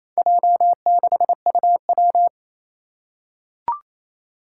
Morse code, heard as a keyed tone, about 28 wpm, 700 Hz. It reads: J6UW
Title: Four Challenging Letter-Number Combinations 28wpm